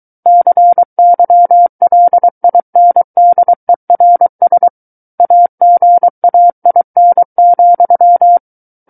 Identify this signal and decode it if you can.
Morse code, heard as a keyed tone, about 23 wpm, 700 Hz. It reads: CYLINDERS AGAIN,
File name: Book One / 14-Chapter 14 - 23 WPM